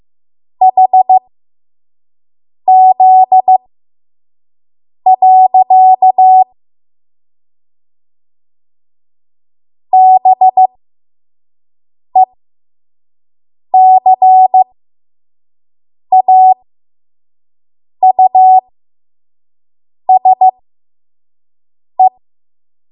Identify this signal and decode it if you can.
Morse code, heard as a keyed tone, about 15 wpm, spaced as for 5 wpm, 750 Hz. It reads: HZ. BECAUSE
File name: Morse / 05WPM